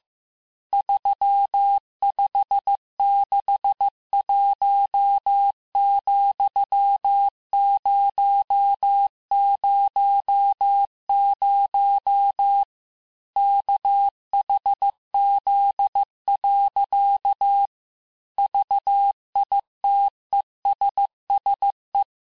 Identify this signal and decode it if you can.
Morse code, heard as a keyed tone, about 15 wpm, 800 Hz. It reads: 3561,000 KHZ. VITESSE
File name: cours-du-lundi files / cours-1585-texte-chiffre-vitesse-14mpm